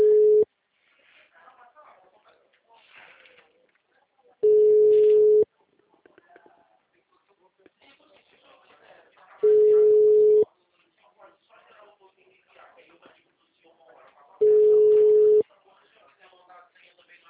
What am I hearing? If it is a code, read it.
Ringback tone (Europe, 425 Hz), with speech.